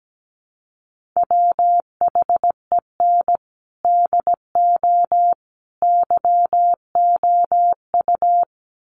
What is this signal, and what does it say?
Morse code, heard as a keyed tone, about 17 wpm, 700 Hz. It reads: WHEN DO YOU
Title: Sentences from Top 400 Words 17wpm